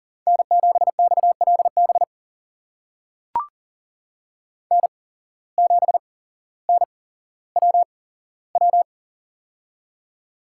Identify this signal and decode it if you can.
Morse code, heard as a keyed tone, about 37 wpm, spaced as for 17 wpm, 700 Hz. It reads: N7XLB N7NWW
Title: USA Call Signs - 5 Characters - 8x Character Spacing - 40wpm